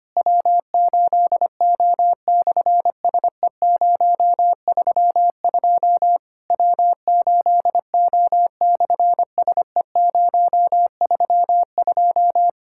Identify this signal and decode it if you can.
Morse code, heard as a keyed tone, about 25 wpm, 700 Hz. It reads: W8O/SE032 W8O/SE032
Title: SOTA QSO Practice 25wpm